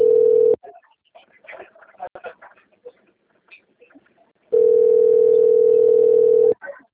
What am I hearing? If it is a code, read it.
Ringback tone (North America), 440 + 480 Hz.
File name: out- - - - - .48805